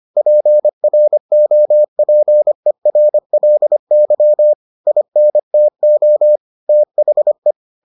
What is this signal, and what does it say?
Morse code, heard as a keyed tone, about 25 wpm, 600 Hz. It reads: PROPERLY INTO THE